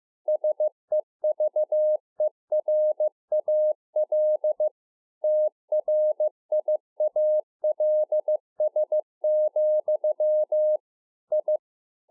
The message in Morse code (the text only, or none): SEVERAL TRIALS, I